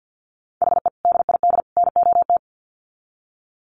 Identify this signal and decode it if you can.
Morse code, heard as a keyed tone, about 50 wpm, 700 Hz. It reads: HE DID NOT
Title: Sets of 3 Words 50wpm